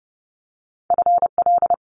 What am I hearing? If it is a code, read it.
Morse code, heard as a keyed tone, about 30 wpm, 700 Hz. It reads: FL
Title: Multi-letter Phonemes Quick Fire 30wpm